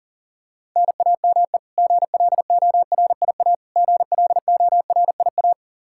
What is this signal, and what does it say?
Morse code, heard as a keyed tone, about 40 wpm, 700 Hz. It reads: NAME GLORIA GLORIA
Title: K1USN SST QSO Practice 40wpm